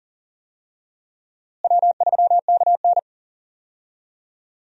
Morse code, heard as a keyed tone, about 40 wpm, 700 Hz. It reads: W3KN
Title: USA Call Signs - 4 Characters - Repeated 2x - 40wpm